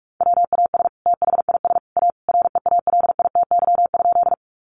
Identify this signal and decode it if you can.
Morse code, heard as a keyed tone, about 44 wpm, 700 Hz. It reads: WAS THIS A REALITY?
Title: Sentences from Top 1200 Words 45wpm